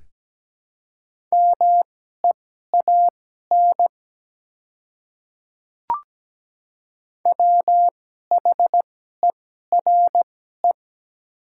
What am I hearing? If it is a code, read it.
Morse code, heard as a keyed tone, about 17 wpm, spaced as for 12 wpm, 700 Hz. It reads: MEAN WHERE
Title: 2x Character Spacing - Top 200 Words - 17wpm